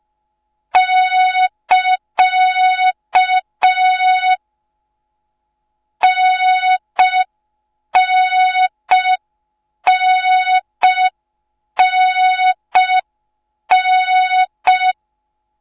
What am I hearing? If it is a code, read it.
Morse code, heard as a keyed tone, about 5 wpm, 3700 Hz. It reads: <KA> NNNNN